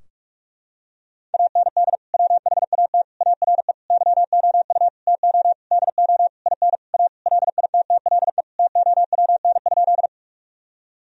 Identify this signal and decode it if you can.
Morse code, heard as a keyed tone, about 46 wpm, 700 Hz. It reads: AND WHAT ARE YOU TO DO IN A LITTLE TOWN?
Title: Sentences from Top 1000 Words - Review Part 2 - 45wpm